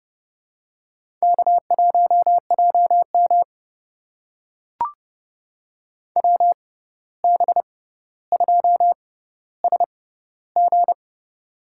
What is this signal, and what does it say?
Morse code, heard as a keyed tone, about 30 wpm, spaced as for 14 wpm, 700 Hz. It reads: K1JM WB2SG
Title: USA Call Signs - 6x Character Spacing - 30wpm